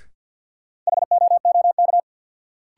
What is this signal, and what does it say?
Morse code, heard as a keyed tone, about 53 wpm, 700 Hz. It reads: HOOK
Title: Sentences from Top 1600 Words - New Word Drill - 50wpm